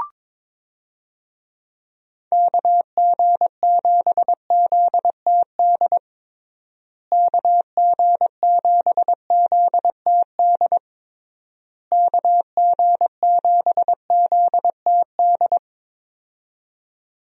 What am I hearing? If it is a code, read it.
Morse code, heard as a keyed tone, about 22 wpm, 700 Hz. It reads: KG7ZTD KG7ZTD KG7ZTD
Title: USA Call Signs - 6 Characters - Repeated 3x - 22wpm